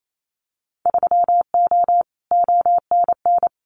Morse code, heard as a keyed tone, about 28 wpm, 700 Hz. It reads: 3O ONN
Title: Winter Field Day Exchanges 28wpm